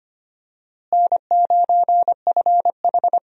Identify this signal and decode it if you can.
Morse code, heard as a keyed tone, about 25 wpm, 700 Hz. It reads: N9FH
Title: USA Call Signs - 4 Characters - Repeated 3x - 25wpm